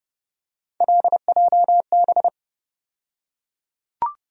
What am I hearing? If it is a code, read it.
Morse code, heard as a keyed tone, about 29 wpm, 700 Hz. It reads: LJB
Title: USA Call Sign Postfixes - 3 Letters 30wpm